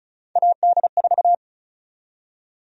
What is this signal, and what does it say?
Morse code, heard as a keyed tone, about 34 wpm, 700 Hz. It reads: AD4
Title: USA Call Sign Prefixes - 35wpm